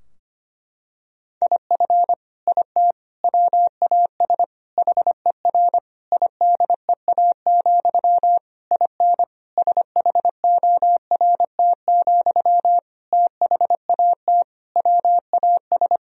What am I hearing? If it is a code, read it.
Morse code, heard as a keyed tone, about 25 wpm, 700 Hz. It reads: IF IT WAS HER IDEA, IN SHORT, THAT WAS
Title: Sentences from Top 1000 Words - Review Part 2 - 25wpm